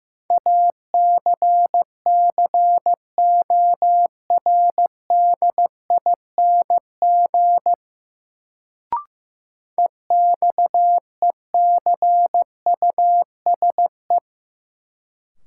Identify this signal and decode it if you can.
Morse code, heard as a keyed tone, about 15 wpm, 700 Hz. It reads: ACCORDING EXECUSE
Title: Sentences from Top 700 Words - New Word Drill - 15wpm